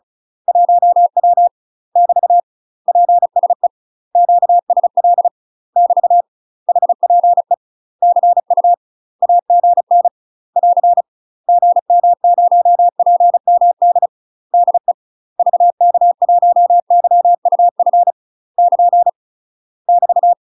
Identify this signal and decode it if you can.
Morse code, heard as a keyed tone, about 34 wpm, 700 Hz. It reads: 1W <BT> PSE QSL <BT> HPE CU AGN <AR> GM0PMD DE VK1YUF <KN> <BT>
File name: UK MASTER QSOs TRAINING / 35WPM / UK MASTER QSOs 0035